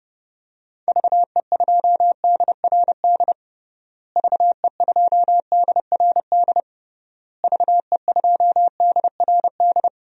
Morse code, heard as a keyed tone, about 30 wpm, 700 Hz. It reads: VE2DRD VE2DRD VE2DRD
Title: DXCC-100 Call Signs - 6 Characters - Repeated 3x - 30wpm